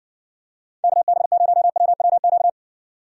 Morse code, heard as a keyed tone, about 60 wpm, 690 Hz. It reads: KB0WWQ